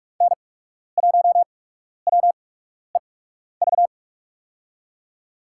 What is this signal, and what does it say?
Morse code, heard as a keyed tone, about 42 wpm, spaced as for 12 wpm, 700 Hz. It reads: N1WEV
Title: USA Call Signs - 8x Character Spacing - 45wpm